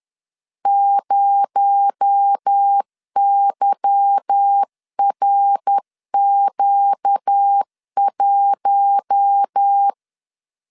Morse code, heard as a keyed tone, about 10 wpm, 800 Hz. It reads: 0YRQ1
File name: LEZIONI / 8 YJ45 / 8 Y 10wpm 2615